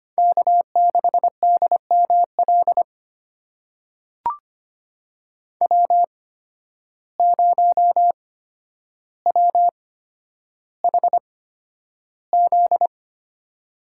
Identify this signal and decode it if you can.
Morse code, heard as a keyed tone, about 25 wpm, spaced as for 11 wpm, 700 Hz. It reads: K6DML W0WHZ
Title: USA Call Signs - 8x Character Spacing - 25wpm